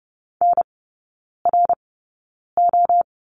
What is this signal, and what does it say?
Morse code, heard as a keyed tone, about 30 wpm, spaced as for 9 wpm, 690 Hz. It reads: NRO
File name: Lesson 16e - Letter c - Three Characters 30wpm